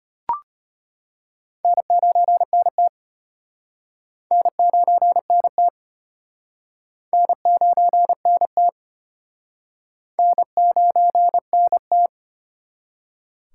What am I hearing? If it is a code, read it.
Morse code, heard as a keyed tone, about 33 wpm, 700 Hz. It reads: N9NT N9NT N9NT N9NT